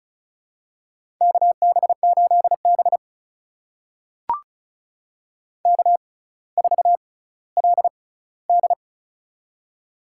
Morse code, heard as a keyed tone, about 36 wpm, spaced as for 18 wpm, 700 Hz. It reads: KB8B K4LD
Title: USA Call Signs - 4 Characters - 6x Character Spacing - 35wpm